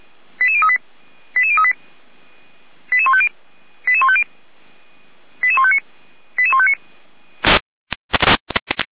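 Five-tone selective call (ZVEI), heard as quick runs of five tones, tones of about 70 ms. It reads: 88038 88038 88150 88150 88159 88159
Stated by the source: ZVEI five-tone selective call received by SDR.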